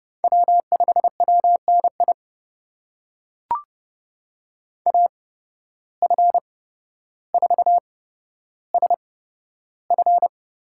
Morse code, heard as a keyed tone, about 31 wpm, spaced as for 13 wpm, 700 Hz. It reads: W5WNI AF4SF
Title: USA Call Signs - 5 Characters - 8x Character Spacing - 30wpm